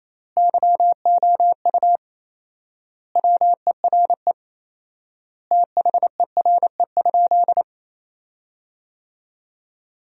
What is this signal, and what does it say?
Morse code, heard as a keyed tone, about 28 wpm, 700 Hz. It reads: YOU WERE THERE?
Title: 4x Word-Spacing - Sentences from Top 500 Words - 28wpm